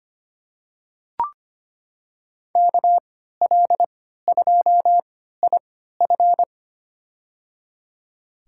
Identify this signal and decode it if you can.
Morse code, heard as a keyed tone, about 25 wpm, spaced as for 14 wpm, 700 Hz. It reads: KL2IF
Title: USA Call Signs - 5 Characters - 3x Character Spacing - 25wpm